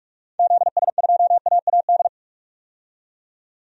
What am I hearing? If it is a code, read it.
Morse code, heard as a keyed tone, about 47 wpm, 700 Hz. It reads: ZS2AAD
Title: DXCC-100 Call Signs - 6 Characters - 45wpm